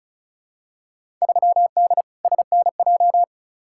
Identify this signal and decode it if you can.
Morse code, heard as a keyed tone, about 36 wpm, 700 Hz. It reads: 3D SNJ
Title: Field Day Exchanges Encore - 35wpm